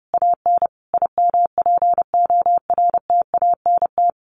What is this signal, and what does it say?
Morse code, heard as a keyed tone, about 30 wpm, 700 Hz. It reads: AN IMPORTANT
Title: A Sound Of Thunder 30wpm